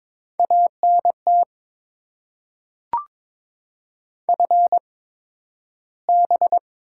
Morse code, heard as a keyed tone, about 22 wpm, spaced as for 12 wpm, 700 Hz. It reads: ANT FB
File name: Top 20 QSO Elements - 8x Character Spacing - 22wpm